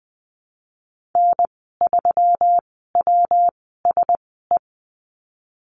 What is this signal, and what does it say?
Morse code, heard as a keyed tone, about 20 wpm, spaced as for 15 wpm, 700 Hz. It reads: N3WSE
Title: 140 Lesson 23g - Letter f - Call Signs 20wpm